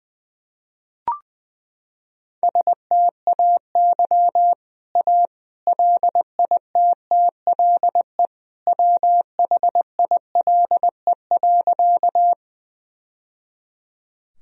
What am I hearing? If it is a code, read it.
Morse code, heard as a keyed tone, about 20 wpm, 700 Hz. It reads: STAY A LITTLE WHILE.
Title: Sentences from Top 1000 Words - Review Part 1 - 20wpm